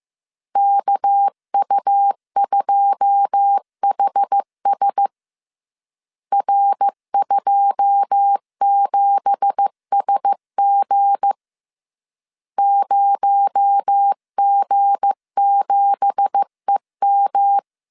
Morse code, heard as a keyed tone, about 15 wpm, 800 Hz. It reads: KU2HS R27SG 0G7EM
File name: CW / DET / 1 9 1500C S A 14wpm 3222